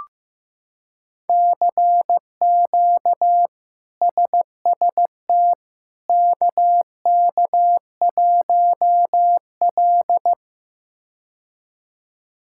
Morse code, heard as a keyed tone, about 15 wpm, 700 Hz. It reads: CQ SST KK1L
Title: K1USN SST QSO Practice - 15wpm